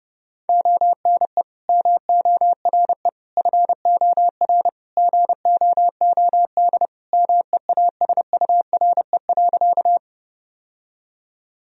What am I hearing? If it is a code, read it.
Morse code, heard as a keyed tone, about 30 wpm, 700 Hz. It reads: ONE MORE FOR GOOD MEASURE.
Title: Sentences from Top 1000 Words - Review Part 2 - 30wpm